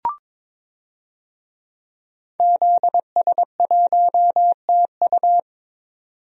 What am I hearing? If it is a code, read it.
Morse code, heard as a keyed tone, about 22 wpm, 700 Hz. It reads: ZS1TU
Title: DXCC-100 Call Signs - 5 Characters - 22wpm